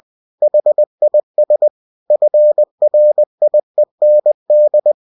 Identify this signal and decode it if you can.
Morse code, heard as a keyed tone, about 20 wpm, 600 Hz. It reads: HIS FRIEND
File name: cw-books / winnie-the-pooh / winnie-20wpm-page-0004